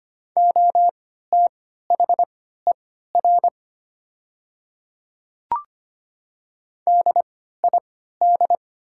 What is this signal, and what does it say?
Morse code, heard as a keyed tone, about 25 wpm, spaced as for 14 wpm, 690 Hz. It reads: OTHER DID
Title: 3x Character Spacing - Top 200 Words - 25wpm